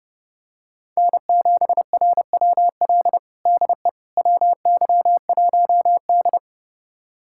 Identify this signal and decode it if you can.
Morse code, heard as a keyed tone, about 30 wpm, 700 Hz. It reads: N7RWL DE WY1D
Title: USA Callsign DE Callsign 30wpm